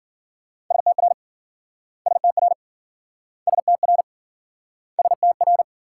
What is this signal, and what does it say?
Morse code, heard as a keyed tone, about 53 wpm, 700 Hz. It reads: STR STR STR STR